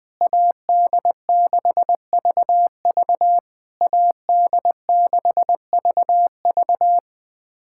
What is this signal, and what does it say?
Morse code, heard as a keyed tone, about 20 wpm, 700 Hz. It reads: AD6VV AD6VV
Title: SOTA QSO Practice 20wpm